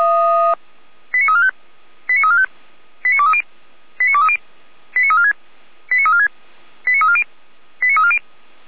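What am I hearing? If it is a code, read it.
Five-tone selective call (ZVEI), heard as quick runs of five tones, tones of about 70 ms. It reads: 89345 89345 89230 89230 89346 89346 89340 89340
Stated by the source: ZVEI five-tone selective call 89112 received by SDR.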